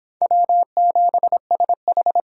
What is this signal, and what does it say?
Morse code, heard as a keyed tone, about 24 wpm, 700 Hz. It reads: W7SH